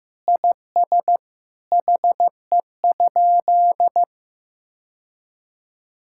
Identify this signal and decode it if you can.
Morse code, heard as a keyed tone, about 15 wpm, 700 Hz. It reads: IS HE?
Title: Sentences from Top 1000 Words - Review Part 5 - 15wpm